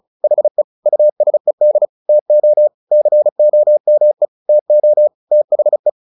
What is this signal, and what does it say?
Morse code, heard as a keyed tone, about 34 wpm, 600 Hz. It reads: HE USED TO COME TO THE
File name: the old man and the sea / the old man and the sea 1 35wpm